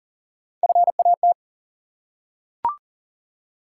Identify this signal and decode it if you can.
Morse code, heard as a keyed tone, about 41 wpm, 700 Hz. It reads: FAT
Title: Three Letter Words - 40wpm